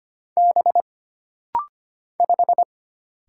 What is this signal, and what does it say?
Morse code, heard as a keyed tone, about 25 wpm, 700 Hz. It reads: B 5
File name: Single Character (Letter:Number:Punctuation:Prosign) - Rapid-Fire - 25wpm